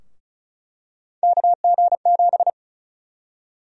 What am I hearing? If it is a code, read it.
Morse code, heard as a keyed tone, about 36 wpm, 700 Hz. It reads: KG7
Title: USA Call Sign Prefixes - Repeated 2x - 35wpm